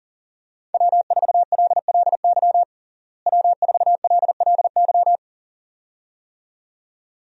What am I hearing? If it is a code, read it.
Morse code, heard as a keyed tone, about 37 wpm, 700 Hz. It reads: W4LLY W4LLY
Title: USA Call Signs - 5 Characters - Repeated 2x - 40wpm